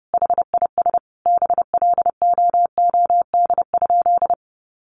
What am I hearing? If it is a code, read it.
Morse code, heard as a keyed tone, about 30 wpm, 700 Hz. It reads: HIS BLOOD?
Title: Sentences from Top 1400 Words 30wpm